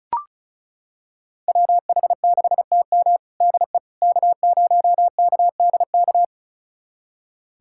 Morse code, heard as a keyed tone, about 36 wpm, 700 Hz. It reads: WH6TM DE K0KDK
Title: USA Callsign DE Callsign - 5 Characters - 35wpm